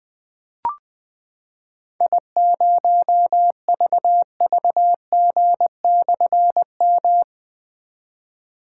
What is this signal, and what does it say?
Morse code, heard as a keyed tone, about 20 wpm, 700 Hz. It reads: I0VVG/M
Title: Difficult Call Signs 20wpm